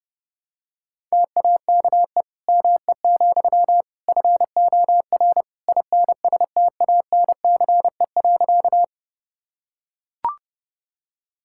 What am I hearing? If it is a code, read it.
Morse code, heard as a keyed tone, about 32 wpm, 700 Hz. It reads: TAKE ME, FOR INSTANCE.
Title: Sentences from Top 1000 Words - Review Part 5 - 30wpm